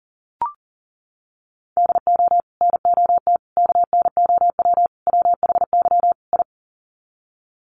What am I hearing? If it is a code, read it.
Morse code, heard as a keyed tone, about 40 wpm, 700 Hz. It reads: DO NOT KNOW WHY I
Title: Sets of 5 Words Encore 40wpm